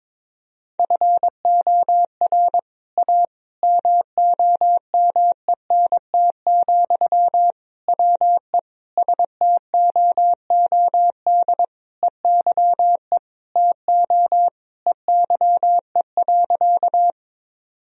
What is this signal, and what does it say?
Morse code, heard as a keyed tone, about 21 wpm, 700 Hz. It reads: FOR A MOMENT, WE STOOD EYE TO EYE.
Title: Sentences from Top 1000 Words - Review Part 2 - 22wpm